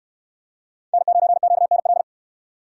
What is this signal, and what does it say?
Morse code, heard as a keyed tone, about 67 wpm, 700 Hz. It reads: N9QNL